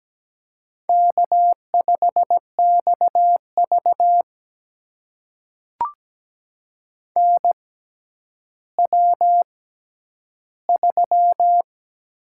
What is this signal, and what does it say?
Morse code, heard as a keyed tone, about 17 wpm, spaced as for 10 wpm, 700 Hz. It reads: K5XV NW3
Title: USA Call Signs - 4 Characters - 6x Character Spacing - 17wpm